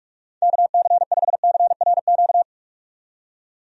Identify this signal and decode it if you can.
Morse code, heard as a keyed tone, about 42 wpm, 700 Hz. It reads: KC5CRQ
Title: USA Call Signs - 6 Characters - 45wpm